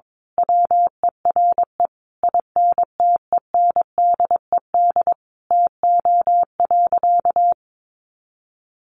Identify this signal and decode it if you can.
Morse code, heard as a keyed tone, about 22 wpm, 700 Hz. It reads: WERE INTENDED TO.
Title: A Sound Of Thunder 22wpm